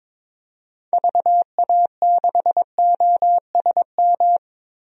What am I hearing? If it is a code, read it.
Morse code, heard as a keyed tone, about 22 wpm, 700 Hz. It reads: VA6OSM
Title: DXCC-100 Call Signs - 6 Characters - Repeated 2x - 22wpm